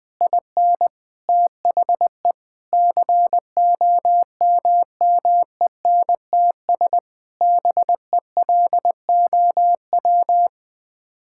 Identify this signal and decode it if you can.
Morse code, heard as a keyed tone, about 20 wpm, 700 Hz. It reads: IN THE COMMENTS BELOW
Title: Sets of 9 Words - 20wpm